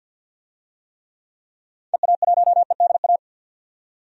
Morse code, heard as a keyed tone, about 52 wpm, 700 Hz. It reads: EA1EDA
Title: DXCC-100 Call Signs - 6 Characters - 50wpm